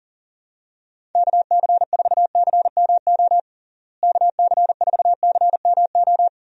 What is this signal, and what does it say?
Morse code, heard as a keyed tone, about 40 wpm, 700 Hz. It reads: KC4CMO KC4CMO
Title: USA Call Signs - 6 Characters - Repeated 3x - 40wpm